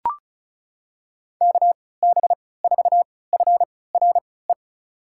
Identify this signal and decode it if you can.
Morse code, heard as a keyed tone, about 36 wpm, spaced as for 20 wpm, 700 Hz. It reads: KB4FRE
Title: USA Call Signs - 6 Characters - 3x Character Spacing - 35wpm